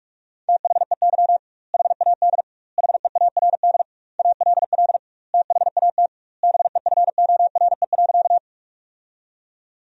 Morse code, heard as a keyed tone, about 46 wpm, 700 Hz. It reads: THEY HAD HEARD ALL THAT BEFORE.
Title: 2x Word-Spacing - Sentences from Top 400 Words - 45wpm